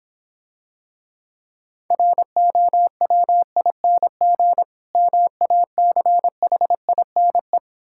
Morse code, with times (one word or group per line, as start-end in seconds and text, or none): 1.9-4.6 ROWING
5.0-7.6 MACHINE